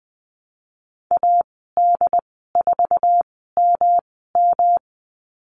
Morse code, heard as a keyed tone, about 20 wpm, spaced as for 15 wpm, 700 Hz. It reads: AD4MM
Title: 128 Lesson 21g - Number 6 - Call Signs 20wpm